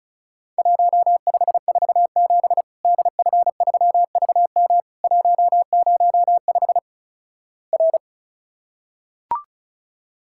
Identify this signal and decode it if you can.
Morse code, heard as a keyed tone, about 36 wpm, 700 Hz. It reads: 1547 DF3VM 105 R
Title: Worked All Europe DX Contest - QTC Exchange - 35wpm